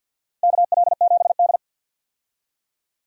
Morse code, heard as a keyed tone, about 48 wpm, 700 Hz. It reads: KL7D
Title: DXCC-100 Call Signs - 4 Characters - Repeated 3x - 50wpm